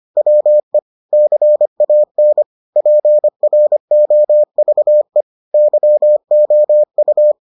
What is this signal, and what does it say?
Morse code, heard as a keyed tone, about 25 wpm, 600 Hz. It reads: WE CAN PROVE YOU